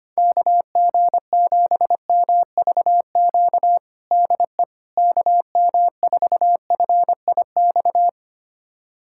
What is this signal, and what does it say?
Morse code, heard as a keyed tone, about 25 wpm, 700 Hz. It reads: KG7MVQ DE KM4FIX
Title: USA Callsign DE Callsign 25wpm